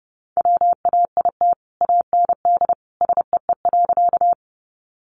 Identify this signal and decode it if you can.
Morse code, heard as a keyed tone, about 28 wpm, 700 Hz. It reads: WAIT AND SEE.
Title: Sentences from Top 500 Words 30wpm